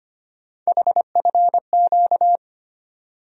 Morse code, heard as a keyed tone, about 25 wpm, 700 Hz. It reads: HFQ
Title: USA Call Sign Postfixes - 3 Letters 25wpm